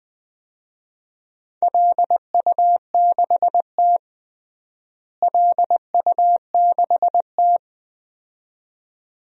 Morse code, heard as a keyed tone, about 21 wpm, 700 Hz. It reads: LU6T LU6T
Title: DXCC-100 Call Signs - 4 Characters - Repeated 2x - 20wpm